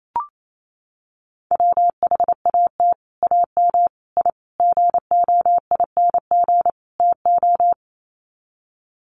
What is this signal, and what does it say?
Morse code, heard as a keyed tone, about 28 wpm, 700 Hz. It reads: WHAT AM I GOING TO
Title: Sets of 5 Words Encore 28wpm